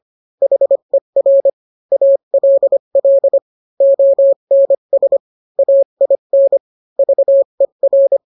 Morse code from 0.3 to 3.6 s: HER ALL